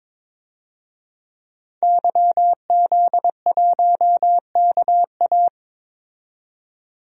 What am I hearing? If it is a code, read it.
Morse code, heard as a keyed tone, about 22 wpm, 700 Hz. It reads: YZ1KA
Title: DXCC-100 Call Signs - 5 Characters - 22wpm